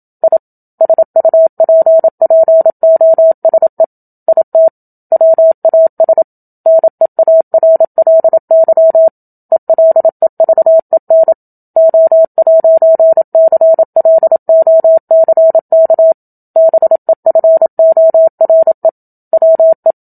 Morse code, heard as a keyed tone, about 27 wpm, 650 Hz. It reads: I SUPPOSE IT WAS NEARLY ELEVEN O'CLOCK BEFORE WE
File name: Book Two / 01-Chapter 1 - 27 WPM